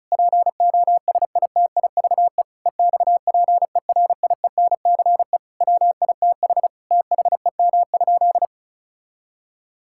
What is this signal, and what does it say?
Morse code, heard as a keyed tone, about 34 wpm, 700 Hz. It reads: POSITIVE EXPERIENCE WITH THEM?
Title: Sentences from Top 1000 Words - Review Part 5 - 35wpm